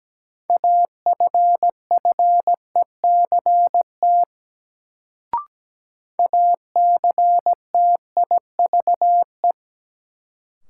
Morse code, heard as a keyed tone, about 17 wpm, 700 Hz. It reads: AFFECT ACTIVE
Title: Sentences from Top 900 Words - New Word Drill - 17wpm